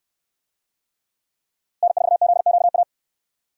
Morse code, heard as a keyed tone, about 68 wpm, 700 Hz. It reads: N4ZQA